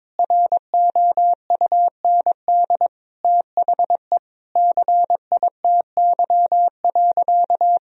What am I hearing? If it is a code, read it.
Morse code, heard as a keyed tone, about 22 wpm, 700 Hz. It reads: ROUND THE CITY.
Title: Sentences from Top 1000 Words - Review Part 1 - 22wpm